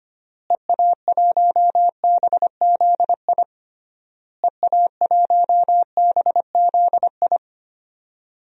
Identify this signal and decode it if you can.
Morse code, heard as a keyed tone, about 25 wpm, 700 Hz. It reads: EA1BZI EA1BZI
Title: DXCC-100 Call Signs - 6 Characters - Repeated 3x - 25wpm